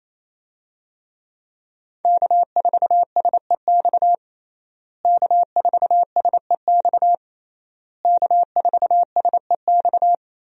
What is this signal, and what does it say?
Morse code, heard as a keyed tone, about 28 wpm, 700 Hz. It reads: K4SEX K4SEX K4SEX
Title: USA Call Signs - 5 Characters - Repeated 3x - 28wpm